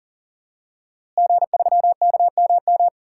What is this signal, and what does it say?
Morse code, heard as a keyed tone, about 39 wpm, 700 Hz. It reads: G3KMM
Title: DXCC-100 Call Signs - 5 Characters - Repeated 3x - 40wpm